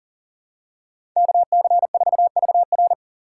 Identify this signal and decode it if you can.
Morse code, heard as a keyed tone, about 37 wpm, 700 Hz. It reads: KC4VR
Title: USA Call Signs - 5 Characters - 4x Character Spacing - 40wpm